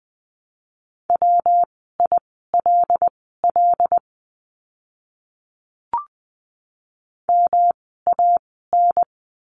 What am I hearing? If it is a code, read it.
Morse code, heard as a keyed tone, about 20 wpm, spaced as for 14 wpm, 700 Hz. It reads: WILL MAN
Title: 115 Lesson 19f - Letter w - Words 20wpm